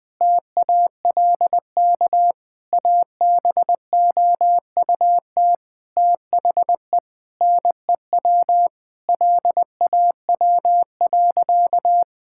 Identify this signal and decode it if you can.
Morse code, heard as a keyed tone, about 20 wpm, 700 Hz. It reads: TALK ABOUT THE NEW LAW.
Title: Sentences with 8-Word Phrases - 20wpm